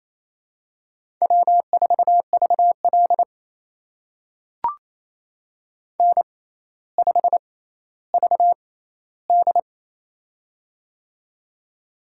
Morse code, heard as a keyed tone, about 28 wpm, spaced as for 14 wpm, 700 Hz. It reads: W4VL N5VD
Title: USA Call Signs - 4 Characters - 6x Character Spacing - 28wpm